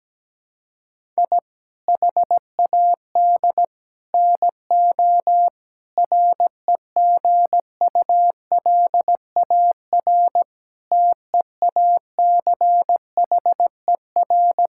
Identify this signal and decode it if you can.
Morse code, heard as a keyed tone, about 17 wpm, 700 Hz. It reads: I HAD NO REGULAR TEACHER
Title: Sentences from Top 1000 Words - Review Part 6 - 17wpm